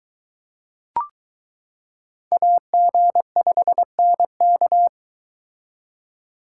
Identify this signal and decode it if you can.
Morse code, heard as a keyed tone, about 22 wpm, 700 Hz. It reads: AG5NK